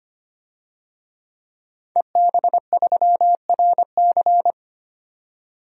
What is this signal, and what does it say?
Morse code, heard as a keyed tone, about 25 wpm, 700 Hz. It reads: EB3RC
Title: DXCC-100 Call Signs - 5 Characters - 25wpm